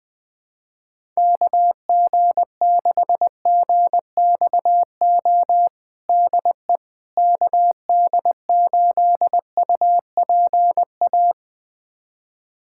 Morse code, heard as a keyed tone, about 20 wpm, 700 Hz. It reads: KG6GXO DE KD8UPA
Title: USA Callsign DE Callsign - 6 Characters - 20wpm